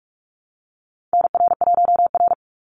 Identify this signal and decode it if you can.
Morse code, heard as a keyed tone, about 44 wpm, 700 Hz. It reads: NR1R
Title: Call Signs Repeated 2x - 45wpm